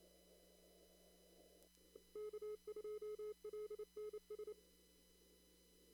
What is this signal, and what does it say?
Morse code, heard as a keyed tone, about 28 wpm, 420 Hz. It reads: K2LNS
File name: CW / kt5j wpx cw-op-k5pi